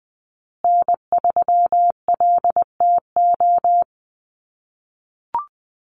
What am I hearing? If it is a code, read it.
Morse code, heard as a keyed tone, about 20 wpm, 700 Hz. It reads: N3LTO
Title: USA Call Signs Repeated 3x 20wpm